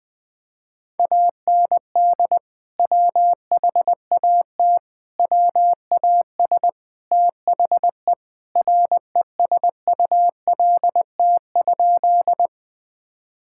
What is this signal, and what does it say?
Morse code, heard as a keyed tone, about 20 wpm, 700 Hz. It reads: AND WHAT WAS THE RESULT?
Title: Sentences from Top 1000 Words - Review Part 5 - 20wpm